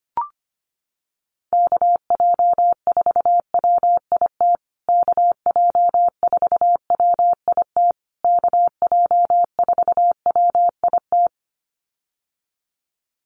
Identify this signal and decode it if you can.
Morse code, heard as a keyed tone, about 25 wpm, 700 Hz. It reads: KJ4WIT KJ4WIT KJ4WIT
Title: USA Call Signs Repeated 3x 25wpm